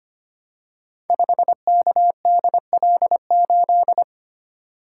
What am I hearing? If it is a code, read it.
Morse code, heard as a keyed tone, about 25 wpm, 700 Hz. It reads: 5KDL8
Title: Code Groups - 5 Letter:Numbers - 25wpm